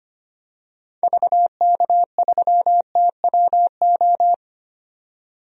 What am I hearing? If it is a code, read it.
Morse code, heard as a keyed tone, about 25 wpm, 700 Hz. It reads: VK3TWO
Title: DXCC-100 Call Signs - 6 Characters - 25wpm